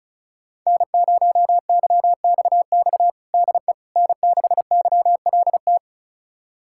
Morse code, heard as a keyed tone, about 36 wpm, 700 Hz. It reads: N0YXX DE N6YLT
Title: USA Callsign DE Callsign - 5 Characters - 35wpm